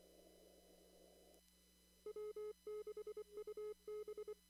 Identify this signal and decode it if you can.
Morse code, heard as a keyed tone, about 23 wpm, 410 Hz. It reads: W6UB
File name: CW / kt5j wpx cw-op-k5pi